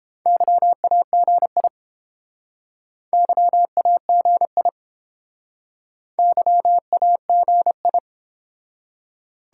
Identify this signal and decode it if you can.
Morse code, heard as a keyed tone, about 30 wpm, 700 Hz. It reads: YAGI YAGI YAGI